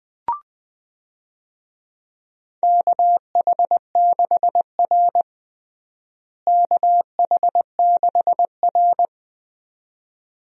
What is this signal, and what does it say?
Morse code, heard as a keyed tone, about 20 wpm, 700 Hz. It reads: KH6R KH6R
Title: DXCC-100 Call Signs - 4 Characters - Repeated 2x - 20wpm